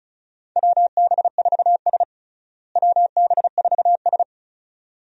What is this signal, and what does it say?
Morse code, heard as a keyed tone, about 36 wpm, 700 Hz. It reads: WB4S WB4S
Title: USA Call Signs - 4 Characters - Repeated 2x - 35wpm